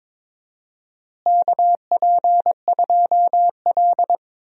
Morse code, heard as a keyed tone, about 21 wpm, 700 Hz. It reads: KP2L
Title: DXCC-100 Call Signs - 4 Characters - Repeated 3x - 22wpm